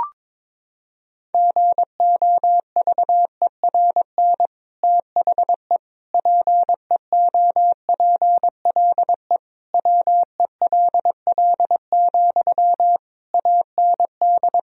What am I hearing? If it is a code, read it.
Morse code, heard as a keyed tone, about 22 wpm, 700 Hz. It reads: GOVERN THE PEOPLE WELL, AND
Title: Sentences from Top 1000 Words - Review Part 2 - 22wpm